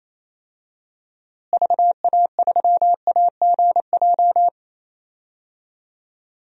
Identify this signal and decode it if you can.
Morse code, heard as a keyed tone, about 28 wpm, 700 Hz. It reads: VA3AGJ
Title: DXCC-100 Call Signs - 6 Characters - Repeated 3x - 28wpm